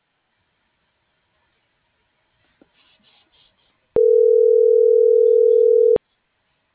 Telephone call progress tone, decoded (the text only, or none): ringback tone